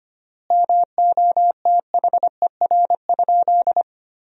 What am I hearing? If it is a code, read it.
Morse code, heard as a keyed tone, about 25 wpm, 700 Hz. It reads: MOTHER?
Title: 3x Word-Spacing - Sentences from Top 200 Words - 25wpm